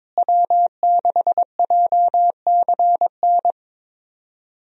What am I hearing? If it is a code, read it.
Morse code, heard as a keyed tone, about 22 wpm, 700 Hz. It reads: W6JCN
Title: USA Call Signs - 5 Characters - Repeated 2x - 22wpm